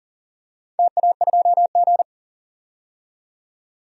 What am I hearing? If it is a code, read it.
Morse code, heard as a keyed tone, about 40 wpm, 700 Hz. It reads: TA2G
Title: DXCC-100 Call Signs - 4 Characters - 40wpm